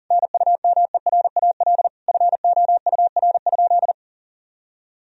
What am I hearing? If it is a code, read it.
Morse code, heard as a keyed tone, about 40 wpm, 700 Hz. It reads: NUMERAL FOUR?
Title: Sentences from Top 1000 Words - Review Part 2 - 40wpm